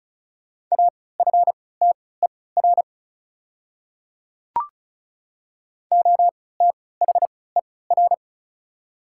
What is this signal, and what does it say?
Morse code, heard as a keyed tone, about 36 wpm, spaced as for 20 wpm, 700 Hz. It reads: AFTER OTHER
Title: 3x Character Spacing - Top 200 Words - 35wpm